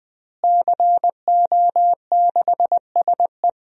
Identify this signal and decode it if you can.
Morse code, heard as a keyed tone, about 21 wpm, 700 Hz. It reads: CO6SE
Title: DXCC-100 Call Signs - 5 Characters - Repeated 2x - 20wpm